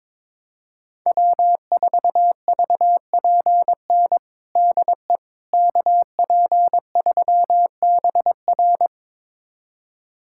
Morse code, heard as a keyed tone, about 22 wpm, 700 Hz. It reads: W4VPN DE KP3BR
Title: USA Callsign DE Callsign - 5 Characters - 22wpm